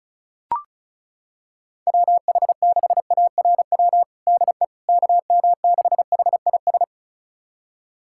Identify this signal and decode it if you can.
Morse code, heard as a keyed tone, about 36 wpm, 700 Hz. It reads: WH6ARW DE KM6HIS
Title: USA Callsign DE Callsign - 6 Characters - 35wpm